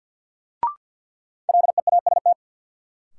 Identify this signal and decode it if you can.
Morse code, heard as a keyed tone, about 52 wpm, 700 Hz. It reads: LEAST
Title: Sets of 2 Words - New Word Drill - 50wpm